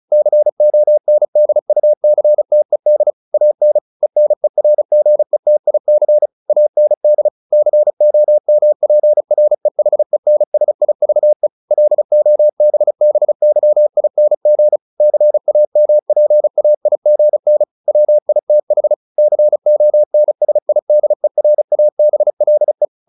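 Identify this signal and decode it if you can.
Morse code, heard as a keyed tone, about 36 wpm, 600 Hz. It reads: CONDUCTED AN ENERGETIC AND COMPREHENSIVE LOBBYING CAMPAIGN WITH CONSIDERABLE
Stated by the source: Morse practice at 35 wpm from CW med Gnister, sending Ken Follett's novel Never in English.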